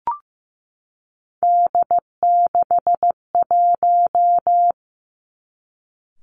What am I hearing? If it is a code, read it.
Morse code, heard as a keyed tone, about 15 wpm, 700 Hz. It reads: D61
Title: Three Challenging Letter-Number Combinations 15wpm